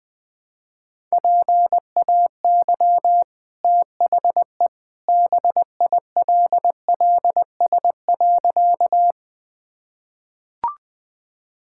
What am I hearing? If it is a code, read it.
Morse code, heard as a keyed tone, about 20 wpm, 700 Hz. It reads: PAY THE BILLS.
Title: Easy Sentences with Hints - 20wpm